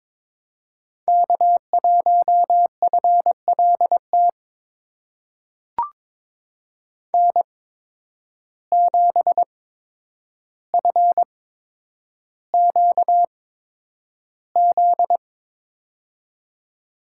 Morse code, heard as a keyed tone, about 22 wpm, spaced as for 9 wpm, 700 Hz. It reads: K1FLT N7FQZ
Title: USA Call Signs - 5 Characters - 8x Character Spacing - 22wpm